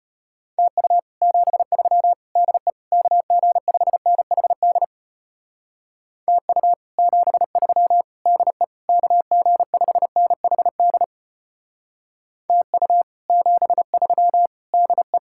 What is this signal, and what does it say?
Morse code, heard as a keyed tone, about 33 wpm, 700 Hz. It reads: TU 73 DE KG5NHD TU 73 DE KG5NHD TU 73 DE